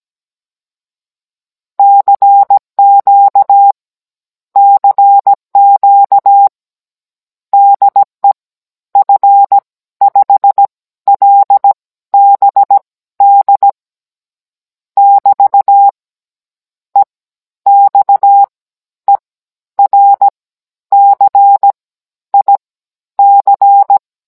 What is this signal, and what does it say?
Morse code, heard as a keyed tone, about 17 wpm, spaced as for 11 wpm, 800 Hz. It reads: CQ CQ DE F5LBD <BT> EXERCIC
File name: Cours-NR18-16mm